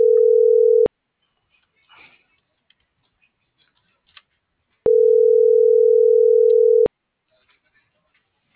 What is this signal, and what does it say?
Ringback tone (North America), 440 + 480 Hz.